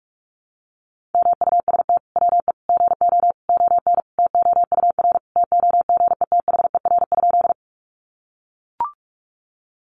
Morse code, heard as a keyed tone, about 44 wpm, 700 Hz. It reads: MUST WE GO ON TOUR TOGETHER?
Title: Sentences from Top 1400 Words 45wpm